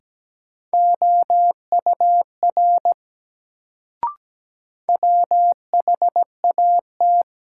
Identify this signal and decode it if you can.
Morse code, heard as a keyed tone, about 17 wpm, 700 Hz. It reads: OUR WHAT
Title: Sets of 2 Words - New Word Drill - 17wpm